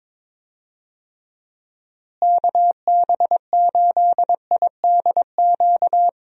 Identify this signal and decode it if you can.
Morse code, heard as a keyed tone, about 22 wpm, 700 Hz. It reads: KB8IDQ
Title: USA Call Signs - 6 Characters - Repeated 3x - 22wpm